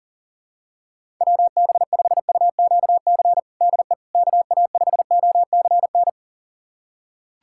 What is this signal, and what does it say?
Morse code, heard as a keyed tone, about 40 wpm, 700 Hz. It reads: WB5UQC DE KA5OCN
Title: USA Callsign DE Callsign - 6 Characters - 40wpm